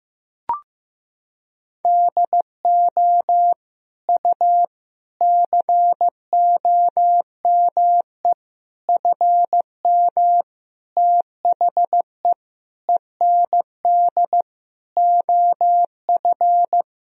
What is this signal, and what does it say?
Morse code, heard as a keyed tone, about 15 wpm, 700 Hz. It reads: DO U COME FM THE END OF
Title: Sentences from Top 200 Words and Abbreviations 15wpm